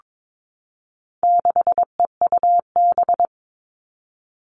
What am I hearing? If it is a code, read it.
Morse code, heard as a keyed tone, about 22 wpm, 700 Hz. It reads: 6EUB
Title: All Cut Numbers - 4-Digit 22wpm